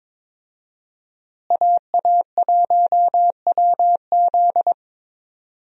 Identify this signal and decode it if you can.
Morse code, heard as a keyed tone, about 22 wpm, 700 Hz. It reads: AA1WZ
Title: USA Call Signs - 5 Characters - Repeated 2x - 22wpm